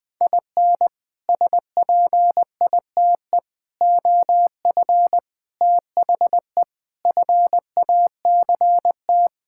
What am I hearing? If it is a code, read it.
Morse code, heard as a keyed tone, about 20 wpm, 700 Hz. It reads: IN SPITE OF THE FACT
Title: Sets of 6 Words - 20wpm